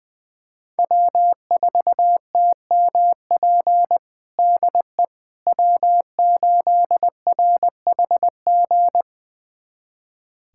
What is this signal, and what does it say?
Morse code, heard as a keyed tone, about 20 wpm, 700 Hz. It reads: W4TMP DE W8RHG
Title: USA Callsign DE Callsign - 5 Characters - 20wpm